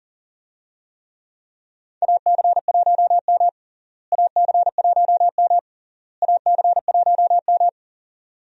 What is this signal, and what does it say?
Morse code, heard as a keyed tone, about 40 wpm, 700 Hz. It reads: AC1M AC1M AC1M
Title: USA Call Signs - 4 Characters - Repeated 3x - 40wpm